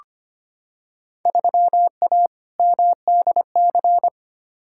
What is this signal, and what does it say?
Morse code, heard as a keyed tone, about 25 wpm, 700 Hz. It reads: 3A MDC
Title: Field Day Exchanges - 25wpm